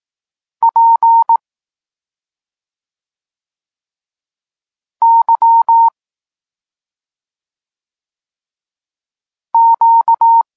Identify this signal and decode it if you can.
Morse code, heard as a keyed tone, about 18 wpm, 920 Hz. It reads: P Y Q
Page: Tekst 32 – Doskonalenie odbioru poznanych liter